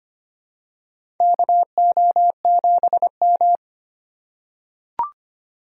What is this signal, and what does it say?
Morse code, heard as a keyed tone, about 25 wpm, 700 Hz. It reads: KO7M
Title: USA Call Signs - 4 Characters - 4x Character Spacing - 25wpm